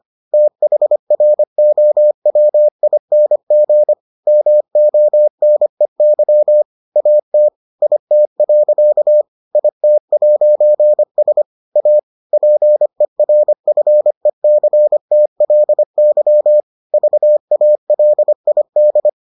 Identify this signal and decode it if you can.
Morse code, heard as a keyed tone, about 25 wpm, 600 Hz. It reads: THROWING MONEY AT IT. IT'S A PERFECTLY VALID